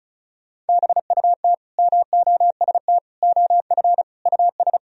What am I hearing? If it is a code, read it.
Morse code, heard as a keyed tone, about 34 wpm, 700 Hz. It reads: BUT MOST OF US
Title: Sentences from Top 1000 Words - Review Part 1 - 35wpm